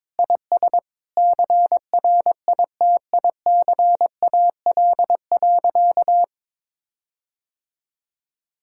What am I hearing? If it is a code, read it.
Morse code, heard as a keyed tone, about 22 wpm, 700 Hz. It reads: IS CRITICAL.
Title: Sentences from Top 1600 Words 22wpm